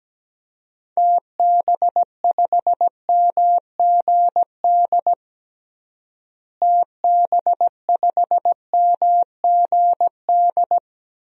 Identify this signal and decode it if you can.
Morse code, heard as a keyed tone, about 17 wpm, 700 Hz. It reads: TB5MGD TB5MGD
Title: DXCC-100 Call Signs - 6 Characters - Repeated 2x - 17wpm